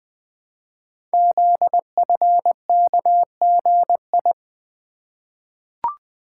Morse code, heard as a keyed tone, about 20 wpm, 700 Hz. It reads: ZFKGI
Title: Code Groups - 5 Letters - 20wpm